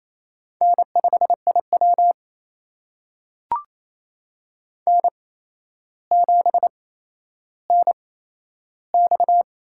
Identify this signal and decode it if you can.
Morse code, heard as a keyed tone, about 28 wpm, spaced as for 12 wpm, 700 Hz. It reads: N5IW N7NX
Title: USA Call Signs - 4 Characters - 8x Character Spacing - 28wpm